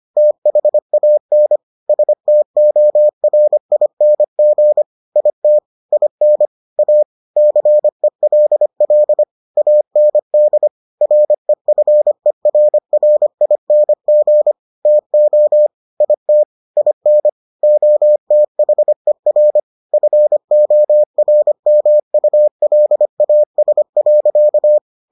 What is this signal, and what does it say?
Morse code, heard as a keyed tone, about 25 wpm, 600 Hz. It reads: THAN STORING IT IN A CELL AND REFERRING TO IT IN OTHER FORMULAS.